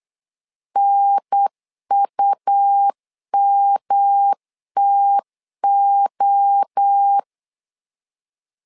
Morse code, heard as a keyed tone, about 8 wpm, 800 Hz. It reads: NUMTO
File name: LEZIONI / 4 Q1S 8wpm 2710